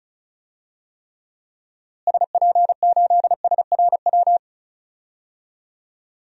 Morse code, heard as a keyed tone, about 34 wpm, 700 Hz. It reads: SP8SRW
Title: DXCC-100 Call Signs - 6 Characters - Repeated 3x - 35wpm